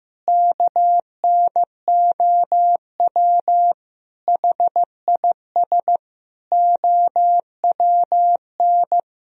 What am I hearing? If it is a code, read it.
Morse code, heard as a keyed tone, about 15 wpm, 700 Hz. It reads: KNOW HIS OWN
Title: Sentences from Top 1000 Words - Review Part 2 - 15wpm